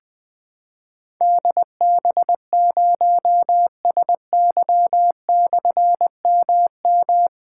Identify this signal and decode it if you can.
Morse code, heard as a keyed tone, about 21 wpm, 700 Hz. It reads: DB0SY/MM
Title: Difficult Call Signs 20wpm